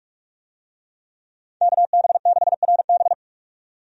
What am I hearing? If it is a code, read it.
Morse code, heard as a keyed tone, about 42 wpm, 700 Hz. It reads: KB6RB
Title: USA Call Signs - 5 Characters - 45wpm